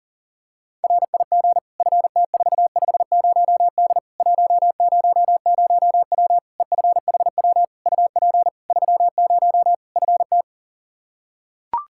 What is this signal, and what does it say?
Morse code, heard as a keyed tone, about 40 wpm, 700 Hz. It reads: RIG FT450D 100W EFHW UP 30 FT
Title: K1USN SST QSO Practice 40wpm